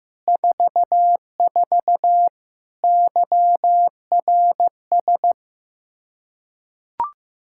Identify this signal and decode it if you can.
Morse code, heard as a keyed tone, about 15 wpm, 700 Hz. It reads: 44 YRS
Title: Practice QSO - Exchange 3 - More Info - 15wpm